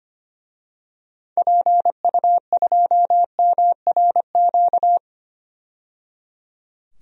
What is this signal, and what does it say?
Morse code, heard as a keyed tone, about 25 wpm, 700 Hz. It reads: PU2MRQ
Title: DXCC-100 Call Signs - 6 Characters - 25wpm